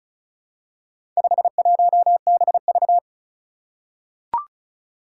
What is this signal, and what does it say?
Morse code, heard as a keyed tone, about 36 wpm, 700 Hz. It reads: 51BV
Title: Four Challenging Letter-Number Combinations 35wpm